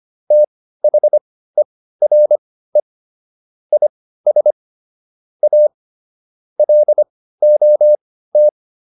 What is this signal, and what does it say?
Morse code, heard as a keyed tone, about 25 wpm, spaced as for 15 wpm, 600 Hz. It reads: THERE IS A LOT